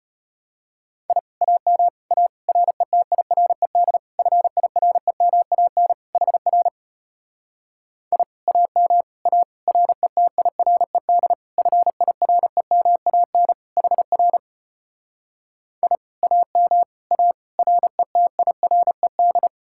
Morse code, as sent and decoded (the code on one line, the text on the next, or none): .. / .- -- / .- / .-. - .. .-. -.. / ..-. .. .-. -- .- -. / .... .-. / .. / .- -- / .- / .-. - .. .-. -.. / ..-. .. .-. -- .- -. / .... .-. / .. / .- -- / .- / .-. - .. .-. -..
I AM A RETIRED FIREMAN HR I AM A RETIRED FIREMAN HR I AM A RETIRED